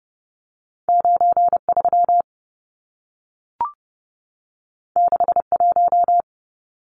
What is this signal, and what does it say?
Morse code, heard as a keyed tone, about 30 wpm, 700 Hz. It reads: 93 61
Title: Numbers - 2-Digit 30wpm